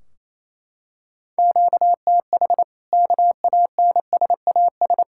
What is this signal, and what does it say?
Morse code, heard as a keyed tone, about 28 wpm, 700 Hz. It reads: QTH KANSAS
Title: Practice QSO - Exchange 1 - Beginning - 28wpm